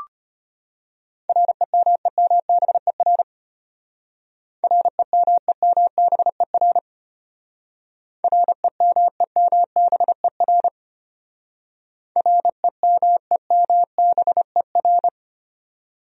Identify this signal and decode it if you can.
Morse code, heard as a keyed tone, about 32 wpm, 700 Hz. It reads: REMEMBER REMEMBER REMEMBER REMEMBER